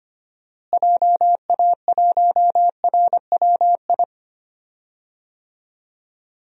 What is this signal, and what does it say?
Morse code, heard as a keyed tone, about 25 wpm, 700 Hz. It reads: JA1RWI
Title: USA Call Signs - 6 Characters - Repeated 2x - 25wpm